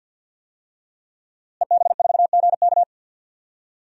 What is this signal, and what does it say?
Morse code, heard as a keyed tone, about 51 wpm, 700 Hz. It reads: EB4GK
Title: DXCC-100 Call Signs - 5 Characters - Repeated 2x - 50wpm